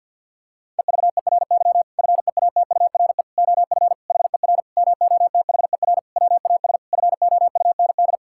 Morse code, heard as a keyed tone, about 50 wpm, 700 Hz. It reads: EVERY FEATURE OF HER MOTHER WAS FOUND
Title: Sentences from Top 1000 Words - Review Part 5 - 50wpm